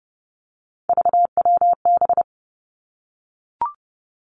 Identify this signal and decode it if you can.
Morse code, heard as a keyed tone, about 30 wpm, 700 Hz. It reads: VWB
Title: Three Challenging Letter-Number Combinations 30wpm